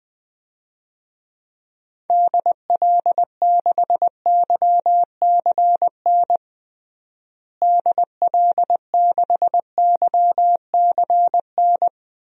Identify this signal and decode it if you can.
Morse code, heard as a keyed tone, about 20 wpm, 700 Hz. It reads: DL6YCN DL6YCN
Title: DXCC-100 Call Signs - 6 Characters - Repeated 3x - 20wpm